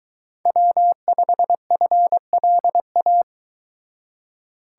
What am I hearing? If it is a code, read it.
Morse code, heard as a keyed tone, about 23 wpm, 700 Hz. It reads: W5FLA